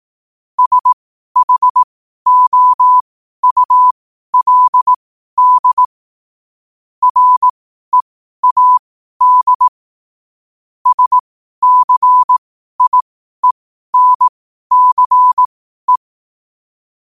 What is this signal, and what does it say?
Morse code, heard as a keyed tone, about 18 wpm, spaced as for 12 wpm, 1000 Hz. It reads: SHOULD READ SCIENCE
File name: qotdfiles / 12wpm